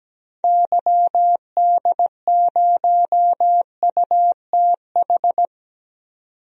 Morse code, heard as a keyed tone, about 17 wpm, 700 Hz. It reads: YD0UTH
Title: DXCC-100 Call Signs - 6 Characters - 17wpm